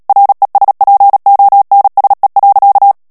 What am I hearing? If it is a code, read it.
Morse code, heard as a keyed tone, about 37 wpm, 800 Hz. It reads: RESPONSE.
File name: morse mp3 / ANGLAIS texte clair 3 vitesse 35 mots minute